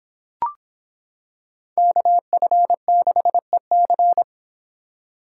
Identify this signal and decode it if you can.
Morse code, heard as a keyed tone, about 26 wpm, 700 Hz. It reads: KF6EC